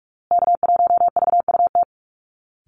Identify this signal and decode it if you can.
Morse code, heard as a keyed tone, about 44 wpm, 700 Hz. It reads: K1VUT
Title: Call Signs Repeated 2x - 45wpm